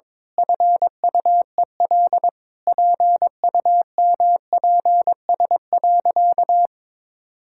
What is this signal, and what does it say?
Morse code, heard as a keyed tone, about 21 wpm, 700 Hz. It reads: FUEL PUMPS.
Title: Pangram Challenge v2 - 22wpm